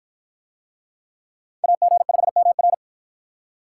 Morse code, heard as a keyed tone, about 52 wpm, 700 Hz. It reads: AG5MR